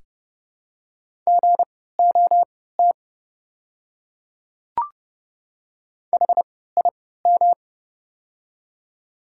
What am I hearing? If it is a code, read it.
Morse code, heard as a keyed tone, about 30 wpm, spaced as for 17 wpm, 700 Hz. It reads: GOT HIM
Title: 3x Character Spacing - Top 300 Words - 30wpm